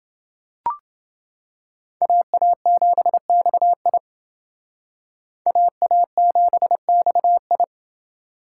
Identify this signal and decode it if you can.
Morse code, heard as a keyed tone, about 28 wpm, 700 Hz. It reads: AA7XI AA7XI